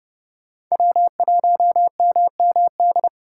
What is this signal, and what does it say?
Morse code, heard as a keyed tone, about 28 wpm, 700 Hz. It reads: W1MMD